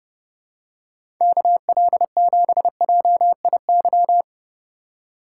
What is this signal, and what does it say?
Morse code, heard as a keyed tone, about 31 wpm, 700 Hz. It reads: KL7JIY
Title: USA Call Signs - 6 Characters - 2x Character Spacing - 30wpm